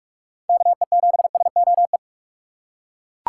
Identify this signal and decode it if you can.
Morse code, heard as a keyed tone, about 42 wpm, 700 Hz. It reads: KE7SOE
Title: USA Call Signs - 6 Characters - 4x Character Spacing - 45wpm